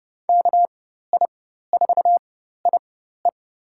Morse code, heard as a keyed tone, about 30 wpm, spaced as for 14 wpm, 700 Hz. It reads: KI4IE
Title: USA Call Signs - 5 Characters - 4x Character Spacing - 30wpm